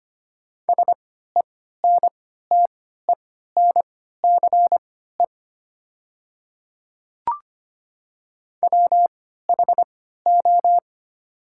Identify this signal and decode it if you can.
Morse code, heard as a keyed tone, about 25 wpm, spaced as for 14 wpm, 690 Hz. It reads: SENTENCE WHO
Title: 3x Character Spacing - Top 200 Words - 25wpm